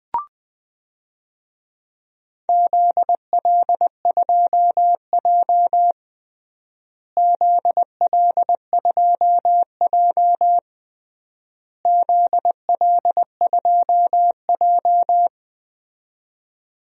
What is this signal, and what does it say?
Morse code, heard as a keyed tone, about 20 wpm, 700 Hz. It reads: ZL2J ZL2J ZL2J
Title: DXCC-100 Call Signs - 4 Characters - Repeated 3x - 20wpm